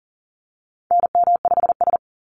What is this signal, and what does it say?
Morse code, heard as a keyed tone, about 41 wpm, 700 Hz. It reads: NM5S
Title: NA SOTA Activator Callsigns 40wpm